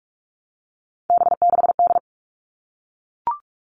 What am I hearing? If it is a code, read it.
Morse code, heard as a keyed tone, about 44 wpm, 700 Hz. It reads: B6D
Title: All Cut Numbers - 3-Digit 45wpm